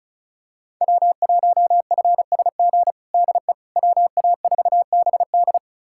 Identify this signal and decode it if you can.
Morse code, heard as a keyed tone, about 36 wpm, 700 Hz. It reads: W1FSG DE WA4BD
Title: USA Callsign DE Callsign - 5 Characters - 35wpm